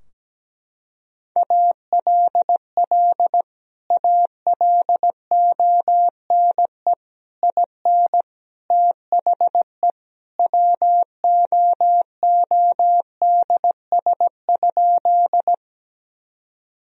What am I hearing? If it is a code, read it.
Morse code, heard as a keyed tone, about 17 wpm, 700 Hz. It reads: ALL ALONE IN THE WOODS?
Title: Sentences from Top 1000 Words - Review Part 2 - 17wpm